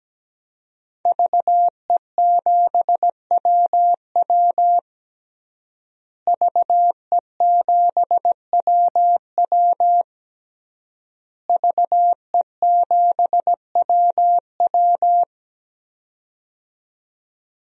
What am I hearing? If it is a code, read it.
Morse code, heard as a keyed tone, about 17 wpm, 700 Hz. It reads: VE7WW VE7WW VE7WW
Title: DXCC-100 Call Signs - 5 Characters - Repeated 3x - 17wpm